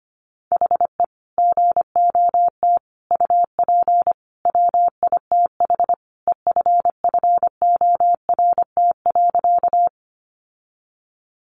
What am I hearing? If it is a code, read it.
Morse code, heard as a keyed tone, about 25 wpm, 700 Hz. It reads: HE GOT UP WITH EFFORT.
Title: Sentences from Top 1200 Words 25wpm